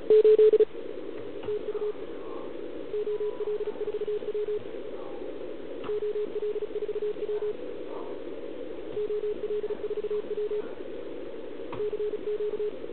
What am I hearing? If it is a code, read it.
Morse code, heard as a keyed tone, about 36 wpm, 410 Hz. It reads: 8 TU OL4W OL4W OL4W CQ